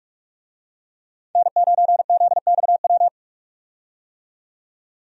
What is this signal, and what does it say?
Morse code, heard as a keyed tone, about 46 wpm, 700 Hz. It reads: N9ZXW
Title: USA Call Signs - 5 Characters - 45wpm